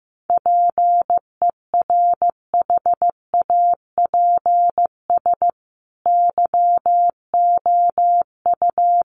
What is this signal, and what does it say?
Morse code, heard as a keyed tone, about 15 wpm, 700 Hz. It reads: PERHAPS YOU
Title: Sentences from Top 1100 Words 15wpm